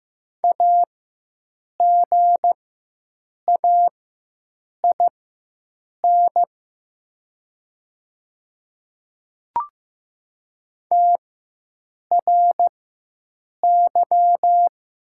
Morse code, heard as a keyed tone, about 15 wpm, spaced as for 7 wpm, 700 Hz. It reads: AGAIN TRY
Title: 4x Character Spacing - Top 300 Words - 15wpm